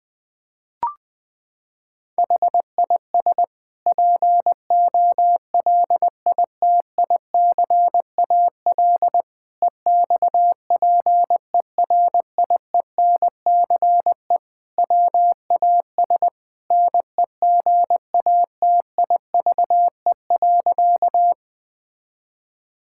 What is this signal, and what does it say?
Morse code, heard as a keyed tone, about 20 wpm, 700 Hz. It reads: HIS POLITICAL EXPERIENCE WAS NEGATIVE.
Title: Sentences from Top 1000 Words - Review Part 6 - 20wpm